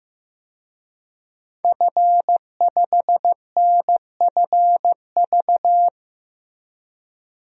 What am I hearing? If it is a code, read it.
Morse code, heard as a keyed tone, about 15 wpm, 700 Hz. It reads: F5NFV
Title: DXCC-100 Call Signs - 5 Characters - Repeated 3x - 15wpm